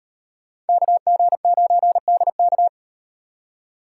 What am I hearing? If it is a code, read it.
Morse code, heard as a keyed tone, about 38 wpm, 700 Hz. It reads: KG9DK